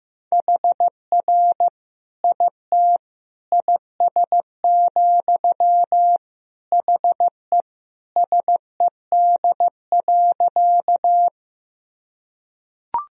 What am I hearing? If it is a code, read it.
Morse code, heard as a keyed tone, about 15 wpm, 700 Hz. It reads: HR IT IS, HE SED.
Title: Sentences from Top 200 Words and Abbreviations 15wpm